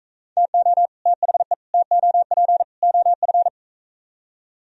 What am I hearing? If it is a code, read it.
Morse code, heard as a keyed tone, about 45 wpm, 700 Hz. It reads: TO THE TOP OF